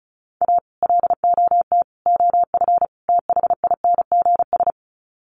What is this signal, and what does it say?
Morse code, heard as a keyed tone, about 34 wpm, 700 Hz. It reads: A LOT OF THINGS
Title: Sets of 4 Words 35wpm